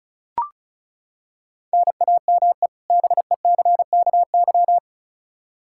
Morse code, heard as a keyed tone, about 36 wpm, 700 Hz. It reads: NAME BECKY
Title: K1USN SST QSO Practice 35wpm